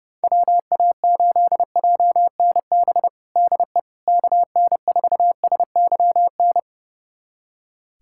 Morse code, heard as a keyed tone, about 30 wpm, 700 Hz. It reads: WA8JNB DE KN4SYN
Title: USA Callsign DE Callsign - 6 Characters - 30wpm